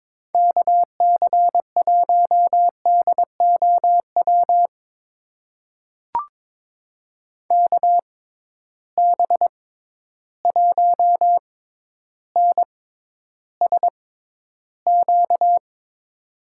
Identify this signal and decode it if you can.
Morse code, heard as a keyed tone, about 22 wpm, spaced as for 11 wpm, 700 Hz. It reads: KC1DOW KB1NSQ
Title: USA Call Signs - 6 Characters - 6x Character Spacing - 22wpm